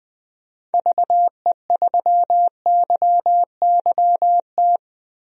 Morse code, heard as a keyed tone, about 20 wpm, 700 Hz. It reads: VE3YYT
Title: DXCC-100 Call Signs - 6 Characters - 20wpm